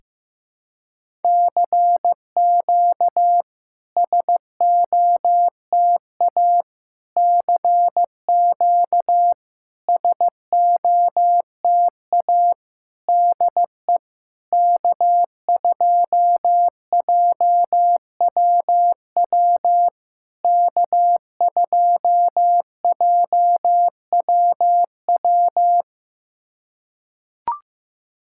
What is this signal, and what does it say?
Morse code, heard as a keyed tone, about 15 wpm, 690 Hz. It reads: CQ SOTA CQ SOTA DE K2JWW K2JWW
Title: SOTA QSO Practice 15wpm